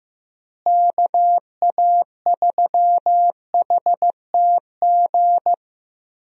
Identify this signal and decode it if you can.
Morse code, heard as a keyed tone, about 16 wpm, 700 Hz. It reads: KA3HTG
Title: USA Call Signs - 6 Characters - Repeated 3x - 15wpm